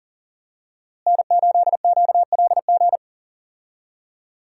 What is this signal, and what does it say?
Morse code, heard as a keyed tone, about 40 wpm, 700 Hz. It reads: N8QLG
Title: USA Call Signs - 5 Characters - Repeated 2x - 40wpm